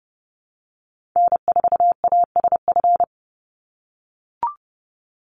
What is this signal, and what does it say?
Morse code, heard as a keyed tone, about 30 wpm, 700 Hz. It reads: N4ASF
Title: Call Signs - 30wpm